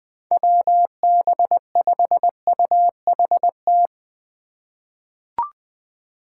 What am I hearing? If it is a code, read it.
Morse code, heard as a keyed tone, about 20 wpm, 700 Hz. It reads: WB5UHT
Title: USA Call Signs - 6 Characters - 4x Character Spacing - 20wpm